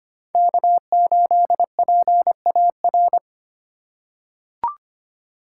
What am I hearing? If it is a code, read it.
Morse code, heard as a keyed tone, about 25 wpm, 700 Hz. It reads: K8PAR
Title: USA Call Signs - 5 Characters - 4x Character Spacing - 25wpm